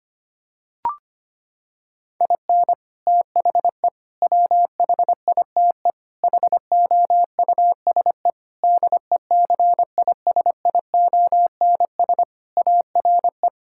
Morse code, heard as a keyed tone, about 25 wpm, 700 Hz. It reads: IN THE WHITE HOUSE DECISIONS ARE
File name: practice / 25wpm / Sentences with 4-Word Phrases - 25wpm